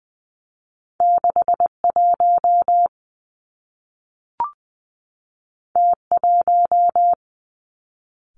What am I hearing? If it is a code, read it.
Morse code, heard as a keyed tone, about 19 wpm, 700 Hz. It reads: 6 1 T 1
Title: Two Challenging Letter-Number Combinations 20wpm